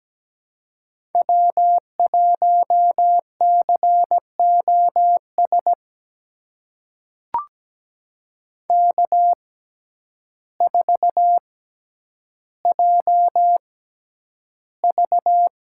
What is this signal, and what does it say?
Morse code, heard as a keyed tone, about 17 wpm, spaced as for 9 wpm, 700 Hz. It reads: W1COS K4JV
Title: USA Call Signs - 5 Characters - 6x Character Spacing - 17wpm